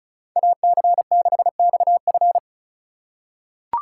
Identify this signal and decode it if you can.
Morse code, heard as a keyed tone, about 34 wpm, 700 Hz. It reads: AC6XF
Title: USA Call Signs - 5 Characters - 3x Character Spacing - 35wpm